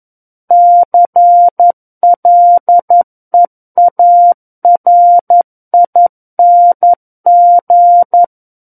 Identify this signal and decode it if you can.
Morse code, heard as a keyed tone, about 11 wpm, 700 Hz. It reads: CLEARING
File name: Book One / 02-Chapter 2 - 11 WPM